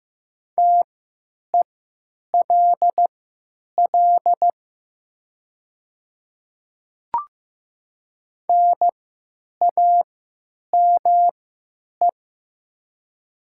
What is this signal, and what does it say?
Morse code, heard as a keyed tone, about 15 wpm, spaced as for 9 wpm, 700 Hz. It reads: TELL NAME
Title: 3x Character Spacing - Top 200 Words - 15wpm